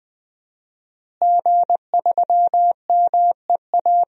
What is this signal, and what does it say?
Morse code, heard as a keyed tone, about 20 wpm, 700 Hz. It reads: G3MEA
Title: DXCC-100 Call Signs - 5 Characters - Repeated 2x - 20wpm